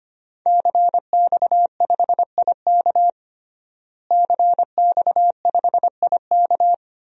Morse code, heard as a keyed tone, about 25 wpm, 700 Hz. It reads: CX5IK CX5IK
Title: DXCC-100 Call Signs - 5 Characters - Repeated 2x - 25wpm